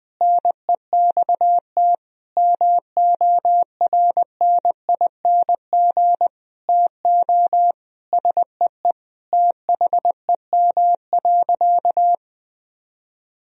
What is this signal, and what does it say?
Morse code, heard as a keyed tone, about 21 wpm, 700 Hz. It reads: NEXT MORNING TO SEE THEM.
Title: Sentences from Top 1000 Words - Review Part 1 - 20wpm